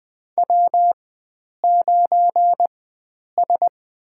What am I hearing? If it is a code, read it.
Morse code, heard as a keyed tone, about 20 wpm, spaced as for 9 wpm, 700 Hz. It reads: W9S
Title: USA Call Signs - 4x Character Spacing - 20wpm